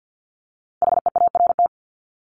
Morse code, heard as a keyed tone, about 50 wpm, 700 Hz. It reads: HEART
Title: Top 1001-1250 Words 50wpm